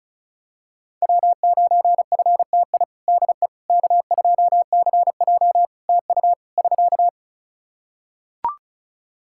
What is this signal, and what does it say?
Morse code, heard as a keyed tone, about 34 wpm, 700 Hz. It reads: W9FTI DE K2CJ TU <SK>
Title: K1USN SST QSO Practice 35wpm